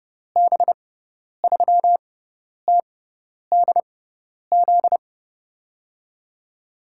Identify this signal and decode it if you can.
Morse code, heard as a keyed tone, about 30 wpm, spaced as for 10 wpm, 700 Hz. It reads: B3TDZ
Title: USA Call Signs - 6 Characters - 6x Character Spacing - 30wpm